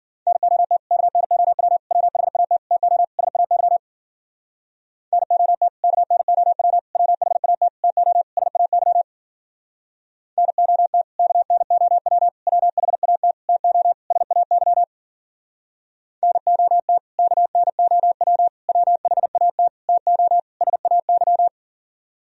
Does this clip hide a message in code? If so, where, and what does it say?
Yes: Morse code from 0.3 to 21.5 s, heard as a keyed tone, about 52 wpm, 700 Hz. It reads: NOT KNOW WHAT TO SAY NOT KNOW WHAT TO SAY NOT KNOW WHAT TO SAY NOT KNOW WHAT TO SAY